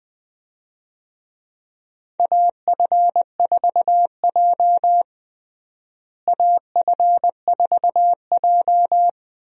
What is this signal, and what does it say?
Morse code, heard as a keyed tone, about 20 wpm, 700 Hz. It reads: AF4J AF4J
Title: USA Call Signs - 4 Characters - Repeated 2x - 20wpm